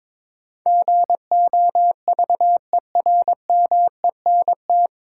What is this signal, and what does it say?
Morse code, heard as a keyed tone, about 23 wpm, 700 Hz. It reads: GOVERMENT
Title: Sentences from Top 1300 Words - New Word Drill - 22wpm